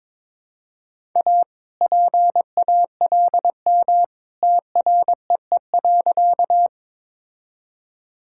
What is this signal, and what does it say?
Morse code, heard as a keyed tone, about 22 wpm, 700 Hz. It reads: A PALM TREE.
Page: morse-code-ninja-practice-22wpm